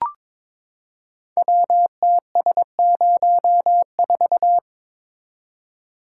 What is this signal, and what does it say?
Morse code, heard as a keyed tone, about 22 wpm, 700 Hz. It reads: WTS04
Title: Code Groups - 5 Letter:Numbers - 22wpm